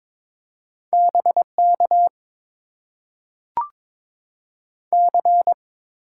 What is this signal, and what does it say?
Morse code, heard as a keyed tone, about 22 wpm, 700 Hz. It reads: BK C
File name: Top 90 QSO Elements - 8x Character Spacing - 22wpm